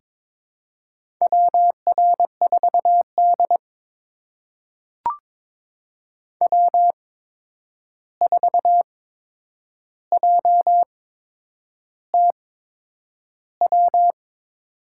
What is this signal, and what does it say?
Morse code, heard as a keyed tone, about 22 wpm, spaced as for 9 wpm, 700 Hz. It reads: WR4D W4JTW
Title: USA Call Signs - 8x Character Spacing - 22wpm